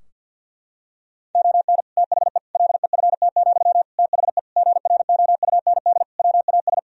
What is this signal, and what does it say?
Morse code, heard as a keyed tone, about 51 wpm, 700 Hz. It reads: ON THE LEFT, THE GROUND WAS
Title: Sentences from Top 1000 Words - Review Part 4 - 50wpm